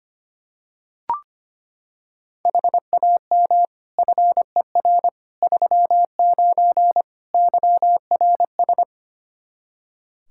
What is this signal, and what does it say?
Morse code, heard as a keyed tone, about 25 wpm, 700 Hz. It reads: HAM FER 39 YRS
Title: Practice QSO - Exchange 3 - More Info - 25wpm